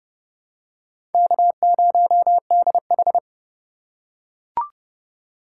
Morse code, heard as a keyed tone, about 30 wpm, 700 Hz. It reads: K0DH
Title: USA Call Signs - 2x Character Spacing - 30wpm